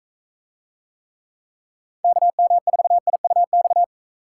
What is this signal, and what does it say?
Morse code, heard as a keyed tone, about 41 wpm, 700 Hz. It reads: KM4IUX